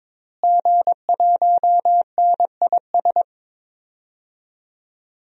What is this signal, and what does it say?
Morse code, heard as a keyed tone, about 22 wpm, 700 Hz. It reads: G1NIS
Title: DXCC-100 Call Signs - 5 Characters - Repeated 3x - 22wpm